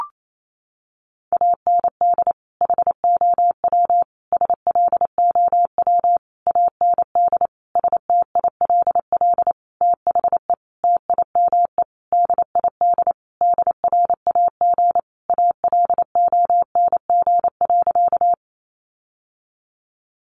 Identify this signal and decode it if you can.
Morse code, heard as a keyed tone, about 28 wpm, 700 Hz. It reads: AND HOW SLOW AND STILL THE TIME DID DRAG ALONG.
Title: Sentences from Top 1500 Words 28wpm